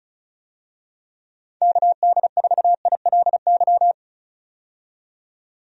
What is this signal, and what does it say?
Morse code, heard as a keyed tone, about 36 wpm, 700 Hz. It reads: KD4ILY
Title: USA Call Signs - 6 Characters - 35wpm